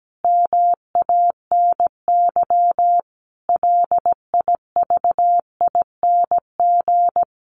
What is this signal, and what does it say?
Morse code, heard as a keyed tone, about 17 wpm, 700 Hz. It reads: MANY LIVING
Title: Sentences from Top 1300 Words 17wpm